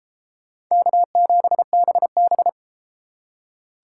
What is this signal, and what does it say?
Morse code, heard as a keyed tone, about 32 wpm, 700 Hz. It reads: K7BB